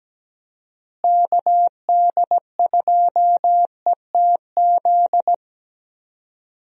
Morse code, heard as a keyed tone, about 17 wpm, 700 Hz. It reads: KD2ETZ
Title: USA Call Signs - 6 Characters - Repeated 3x - 17wpm